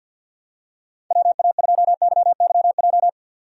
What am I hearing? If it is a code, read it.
Morse code, heard as a keyed tone, about 46 wpm, 700 Hz. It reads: WA2YYJ
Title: USA Call Signs - 6 Characters - 50wpm